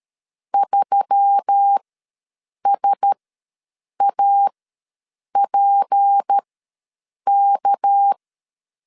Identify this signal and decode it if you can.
Morse code, heard as a keyed tone, about 13 wpm, spaced as for 7 wpm, 800 Hz. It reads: 3SAPK
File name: LEZIONI / 6 PKX0 / 6 PKX ws 12wpm 3106